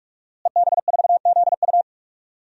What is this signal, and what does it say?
Morse code, heard as a keyed tone, about 46 wpm, 700 Hz. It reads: EB4ZU
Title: DXCC-100 Call Signs - 5 Characters - Repeated 2x - 45wpm